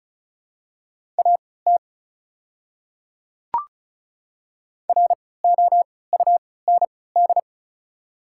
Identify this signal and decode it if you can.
Morse code, heard as a keyed tone, about 36 wpm, spaced as for 20 wpm, 700 Hz. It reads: AT ROUND
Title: 3x Character Spacing - Top 300 Words - 35wpm